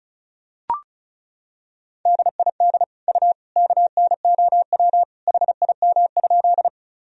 Morse code, heard as a keyed tone, about 34 wpm, 700 Hz. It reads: DID U KNOW HIM?
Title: Sentences from Top 100 Words and Abbreviations 35wpm